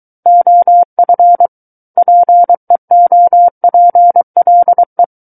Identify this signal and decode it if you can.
Morse code, heard as a keyed tone, about 23 wpm, 700 Hz. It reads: OF PEOPLE
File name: Book One / 14-Chapter 14 - 23 WPM